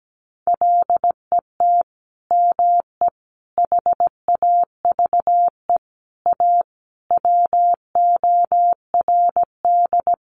Morse code, heard as a keyed tone, about 17 wpm, 700 Hz. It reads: LET ME HAVE A WORD
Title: Sentences from Top 900 Words 17wpm